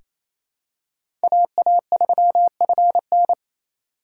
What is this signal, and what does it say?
Morse code, heard as a keyed tone, about 28 wpm, 700 Hz. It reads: AA3FN
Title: USA Call Signs - 5 Characters - 2x Character Spacing - 28wpm